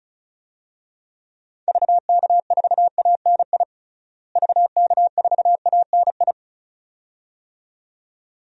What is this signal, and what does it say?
Morse code, heard as a keyed tone, about 34 wpm, 700 Hz. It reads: VK4ANI VK4ANI
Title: DXCC-100 Call Signs - 6 Characters - Repeated 2x - 35wpm